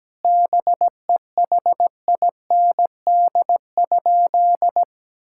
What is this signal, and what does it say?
Morse code, heard as a keyed tone, about 17 wpm, 700 Hz. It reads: BEHIND?
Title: Sentences from Top 1000 Words - Review Part 5 - 17wpm